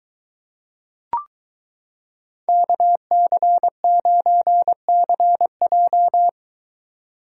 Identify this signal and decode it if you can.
Morse code, heard as a keyed tone, about 23 wpm, 700 Hz. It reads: KC9CJ